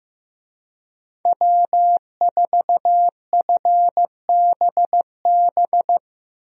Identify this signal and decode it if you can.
Morse code, heard as a keyed tone, about 15 wpm, 700 Hz. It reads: W4FBB
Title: USA Call Signs - 5 Characters - Repeated 2x - 15wpm